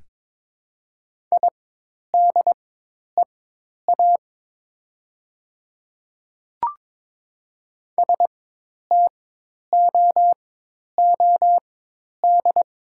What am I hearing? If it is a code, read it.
Morse code, heard as a keyed tone, about 21 wpm, spaced as for 10 wpm, 700 Hz. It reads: IDEA STOOD
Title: 4x Character Spacing - Top 500 Words - 22wpm